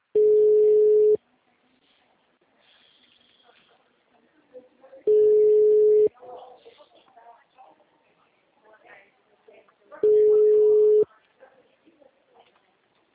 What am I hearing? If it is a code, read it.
Ringback tone (Europe, 425 Hz), with speech.